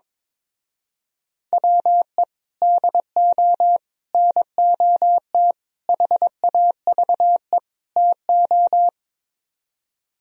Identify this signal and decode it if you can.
Morse code, heard as a keyed tone, about 22 wpm, 700 Hz. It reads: WE DO NOT HAVE TO
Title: Speed-Racing - Sets of 5 Words 22wpm